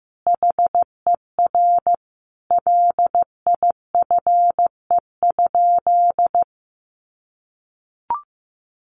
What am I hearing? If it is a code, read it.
Morse code, heard as a keyed tone, about 15 wpm, 700 Hz. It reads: HER LIFE?
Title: Sentences from Top 1500 Words 15wpm